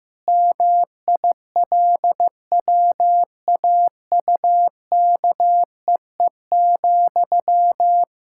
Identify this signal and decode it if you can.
Morse code, heard as a keyed tone, about 15 wpm, 700 Hz. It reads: MILWAUKEE,
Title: Practice QSO - Exchange 1 - Beginning - 15wpm